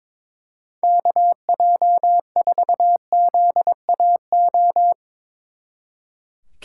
Morse code, heard as a keyed tone, about 25 wpm, 700 Hz. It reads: KJ4ZAO
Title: Speed Racing - USA 2x3 Call Signs 22wpm